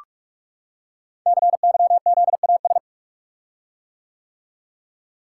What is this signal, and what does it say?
Morse code, heard as a keyed tone, about 44 wpm, 700 Hz. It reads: CYZAS
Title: Code Groups - 5 Letters - 45wpm